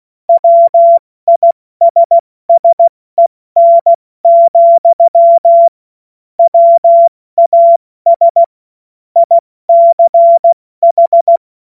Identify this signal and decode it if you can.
Morse code, heard as a keyed tone, about 16 wpm, 680 Hz. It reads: WISSEN, WAS ICH
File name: Morsekurs / Witze / Witz 24. Januar 2023 70 F80